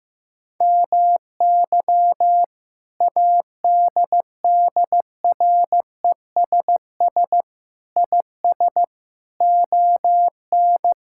Morse code, heard as a keyed tone, about 15 wpm, 700 Hz. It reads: MY ADDRESS IS ON
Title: Sentences from Top 1000 Words - Review Part 6 - 15wpm